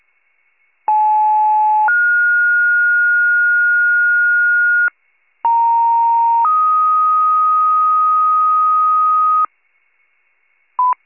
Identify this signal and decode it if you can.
Two-tone page, heard as a one-second tone, then a three-second tone, about 860 then 1430 Hz, about 930 then 1250 Hz. It is heard from 0.9 to 9.5 s.
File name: pager / 75 FIRE / wcalert